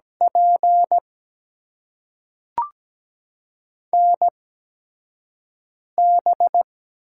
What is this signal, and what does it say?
Morse code, heard as a keyed tone, about 17 wpm, spaced as for 5 wpm, 700 Hz. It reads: P NB
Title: USA Call Signs - 4 Characters - 8x Character Spacing - 17wpm